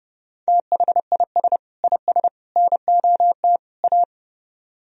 Morse code, heard as a keyed tone, about 37 wpm, 700 Hz. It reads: THIS IS NOT A
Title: Speed-Racing - Sets of 4 Words 30wpm